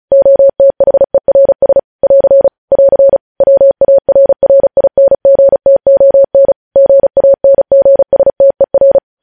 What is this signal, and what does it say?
Morse code, heard as a keyed tone, about 36 wpm, 550 Hz. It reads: OTHERS <AR> <AR> WARRINGTON GANGSTER
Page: News headlines in Morse code at 35 wpm